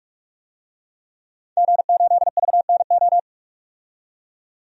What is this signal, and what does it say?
Morse code, heard as a keyed tone, about 46 wpm, 700 Hz. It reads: G8VNO
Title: DXCC-100 Call Signs - 5 Characters - 45wpm